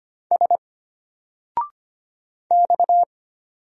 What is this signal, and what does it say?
Morse code, heard as a keyed tone, about 25 wpm, 700 Hz. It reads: S X
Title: Single Letter-Number Alt 25wpm